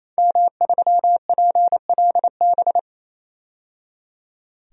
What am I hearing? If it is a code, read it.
Morse code, heard as a keyed tone, about 28 wpm, 700 Hz. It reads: M3PLB
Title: DXCC-100 Call Signs - 5 Characters - 28wpm